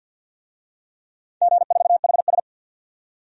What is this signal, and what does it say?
Morse code, heard as a keyed tone, about 47 wpm, 700 Hz. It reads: G4HS
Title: DXCC-100 Call Signs - 4 Characters - 50wpm